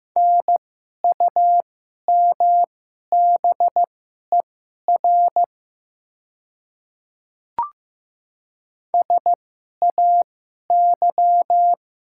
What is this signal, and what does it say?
Morse code, heard as a keyed tone, about 15 wpm, spaced as for 11 wpm, 700 Hz. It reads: NUMBER SAY
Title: 2x Character Spacing - Top 200 Words - 15wpm